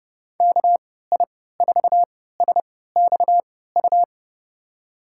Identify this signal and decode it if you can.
Morse code, heard as a keyed tone, about 32 wpm, spaced as for 17 wpm, 700 Hz. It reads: KI4SXU
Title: USA Call Signs - 6 Characters - 3x Character Spacing - 30wpm